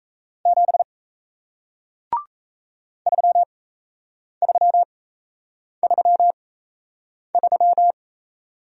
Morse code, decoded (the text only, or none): 7 3333